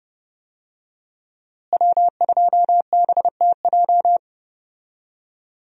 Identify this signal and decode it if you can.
Morse code, heard as a keyed tone, about 30 wpm, 700 Hz. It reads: W2BTJ
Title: USA Call Signs - 5 Characters - Repeated 2x - 30wpm